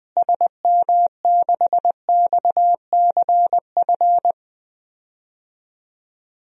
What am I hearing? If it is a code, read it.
Morse code, heard as a keyed tone, about 20 wpm, 700 Hz. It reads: SM6XCF
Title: DXCC-100 Call Signs - 6 Characters - Repeated 3x - 20wpm